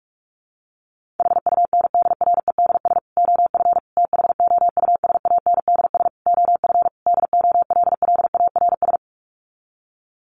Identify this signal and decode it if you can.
Morse code, heard as a keyed tone, about 44 wpm, 700 Hz. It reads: HUNDREDS OF THOUSANDS OF DOLLARS
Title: Sets of 5 Words Encore 45wpm